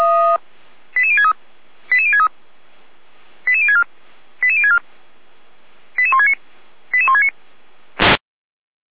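Five-tone selective call (ZVEI), heard as quick runs of five tones, tones of about 70 ms. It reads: 88063 88063 88064 88064 88169 88169
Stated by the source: ZVEI five-tone selective call received on SDR.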